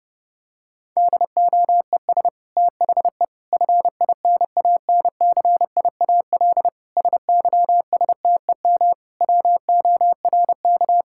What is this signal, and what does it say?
Morse code, heard as a keyed tone, about 31 wpm, 700 Hz. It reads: DOES THE FINANCIAL SYSTEM WORK
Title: Sentences from Top 1000 Words - Review Part 5 - 30wpm